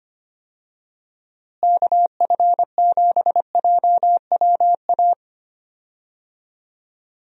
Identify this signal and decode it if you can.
Morse code, heard as a keyed tone, about 25 wpm, 700 Hz. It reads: KF7JWA
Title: USA Call Signs - 6 Characters - Repeated 2x - 25wpm